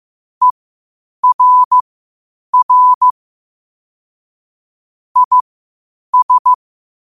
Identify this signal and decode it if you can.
Morse code, heard as a keyed tone, about 15 wpm, spaced as for 8 wpm, 1000 Hz. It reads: ERR IS
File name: qotdfiles / 8wpm